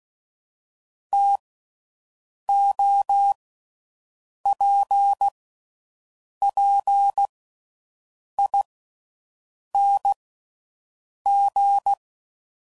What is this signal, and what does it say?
Morse code, heard as a keyed tone, about 16 wpm, spaced as for 6 wpm, 800 Hz. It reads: TOPPING
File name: Mixed / Variable Sp / TOs